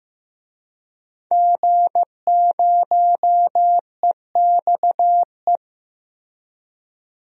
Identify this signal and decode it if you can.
Morse code, heard as a keyed tone, about 14 wpm, 700 Hz. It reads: G0EXE
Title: DXCC-100 Call Signs - 5 Characters - Repeated 2x - 15wpm